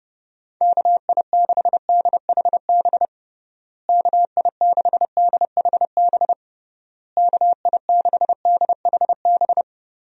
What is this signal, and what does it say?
Morse code, heard as a keyed tone, about 30 wpm, 700 Hz. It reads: KI6DHB KI6DHB KI6DHB
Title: USA Call Signs - 6 Characters - Repeated 3x - 30wpm